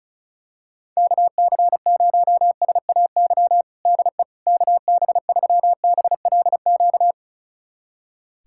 Morse code, heard as a keyed tone, about 36 wpm, 690 Hz. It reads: KC0SAY DE KB3BLQ
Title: USA Callsign DE Callsign - 6 Characters - 35wpm